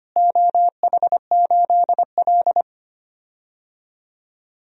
Morse code, heard as a keyed tone, about 25 wpm, 700 Hz. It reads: OH8L
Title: DXCC-100 Call Signs - 4 Characters - Repeated 3x - 25wpm